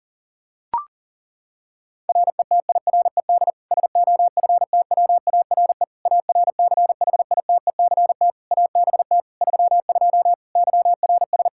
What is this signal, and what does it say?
Morse code, heard as a keyed tone, about 37 wpm, 700 Hz. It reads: RETIRED SOFTWARE ARCHITECT ABT 32 YRS
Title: Practice QSO - Exchange 3 - More Info - 40wpm